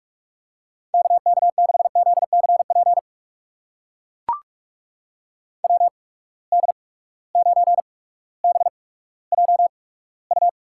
Morse code, heard as a keyed tone, about 44 wpm, spaced as for 19 wpm, 700 Hz. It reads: KK6ZCP WD9BJU
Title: USA Call Signs - 8x Character Spacing - 45wpm